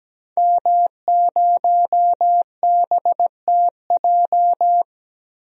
Morse code, heard as a keyed tone, about 17 wpm, 700 Hz. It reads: M0BTJ
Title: DXCC-100 Call Signs - 5 Characters - Repeated 3x - 17wpm